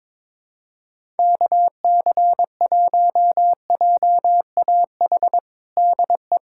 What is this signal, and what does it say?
Morse code, heard as a keyed tone, about 22 wpm, 700 Hz. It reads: KC1JAH DE
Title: USA Callsign DE Callsign 22wpm